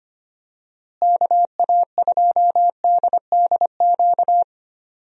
Morse code, heard as a keyed tone, about 25 wpm, 700 Hz. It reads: KA2DDQ
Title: USA Call Signs - 6 Characters - Repeated 3x - 25wpm